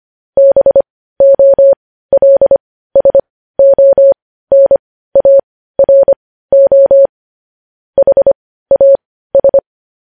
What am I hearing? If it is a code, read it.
Morse code, heard as a keyed tone, about 25 wpm, spaced as for 15 wpm, 550 Hz. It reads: BOLSONARO HAS
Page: News Headlines in Morse Code at 15 WPM